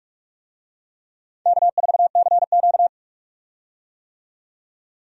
Morse code, heard as a keyed tone, about 43 wpm, 700 Hz. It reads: K4CQ
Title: USA Call Signs - 4 Characters - 45wpm